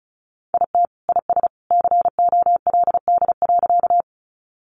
Morse code, heard as a keyed tone, about 34 wpm, 700 Hz. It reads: IT IS COLD.
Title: Sentences from Top 400 Words 35wpm